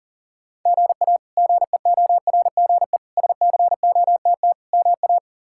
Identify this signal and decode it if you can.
Morse code, heard as a keyed tone, about 40 wpm, 700 Hz. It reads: GA GEORGE SCOTT MA
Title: K1USN SST QSO Practice - 40wpm